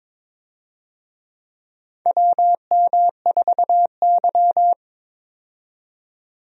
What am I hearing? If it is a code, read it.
Morse code, heard as a keyed tone, about 22 wpm, 700 Hz. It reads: WM4Y
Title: USA Call Signs - 4 Characters - 22wpm